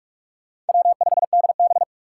Morse code, heard as a keyed tone, about 46 wpm, 700 Hz. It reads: W5DB
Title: USA Call Signs - 4 Characters - 6x Character Spacing - 45wpm